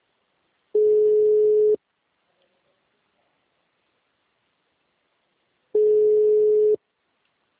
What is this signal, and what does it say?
Ringback tone (Europe, 425 Hz).